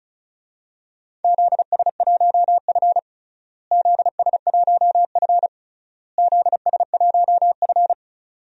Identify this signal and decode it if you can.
Morse code, heard as a keyed tone, about 36 wpm, 700 Hz. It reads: ZS1F ZS1F ZS1F
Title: DXCC-100 Call Signs - 4 Characters - Repeated 3x - 35wpm